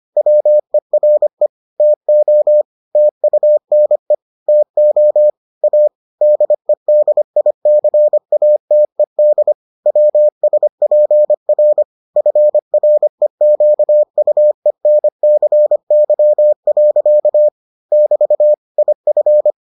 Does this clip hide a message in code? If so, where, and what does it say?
Yes: Morse code from 0.2 to 19.5 s, heard as a keyed tone, about 25 wpm, 600 Hz. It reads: WERE TO TUNE TO A DEDICATED WSPR FREQUENCY. <BT> IF